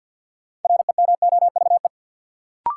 Morse code, heard as a keyed tone, about 50 wpm, 700 Hz. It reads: REMOVE
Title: Sentences from Top 800 Words - New Word Drill - 50wpm